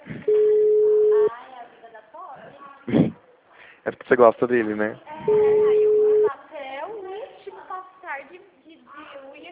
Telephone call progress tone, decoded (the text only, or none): ringback tone